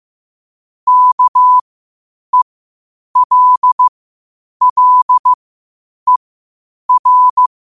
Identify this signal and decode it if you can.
Morse code, heard as a keyed tone, about 15 wpm, spaced as for 8 wpm, 1000 Hz. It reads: KELLER
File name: qotdfiles / 8wpm